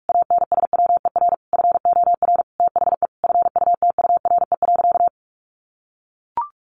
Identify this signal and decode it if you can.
Morse code, heard as a keyed tone, about 44 wpm, 700 Hz. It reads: ANSWER FOR THE FUTURE.
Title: Sentences from Top 1400 Words 45wpm